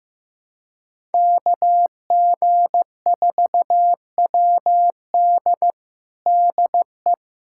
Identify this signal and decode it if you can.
Morse code, heard as a keyed tone, about 16 wpm, 700 Hz. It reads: KG4WD DE
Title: USA Callsign DE Callsign - 5 Characters - 15wpm